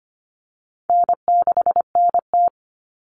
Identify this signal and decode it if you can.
Morse code, heard as a keyed tone, about 25 wpm, 700 Hz. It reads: N6NT
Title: Common Cut Numbers - 4-Digit 25wpm